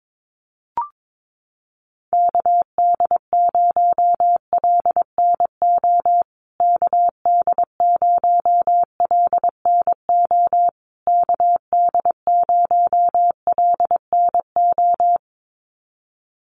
Morse code, heard as a keyed tone, about 22 wpm, 700 Hz. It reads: KD0LNO KD0LNO KD0LNO
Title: USA Call Signs Repeated 3x 22wpm